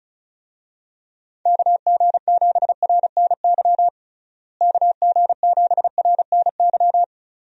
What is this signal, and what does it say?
Morse code, heard as a keyed tone, about 34 wpm, 700 Hz. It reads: KG7RNY KG7RNY
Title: USA Call Signs - 6 Characters - Repeated 3x - 35wpm